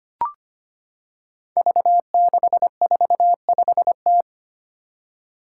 Morse code, heard as a keyed tone, about 25 wpm, 700 Hz. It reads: V645T
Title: Five Challenging Letter-Number Combinations 25wpm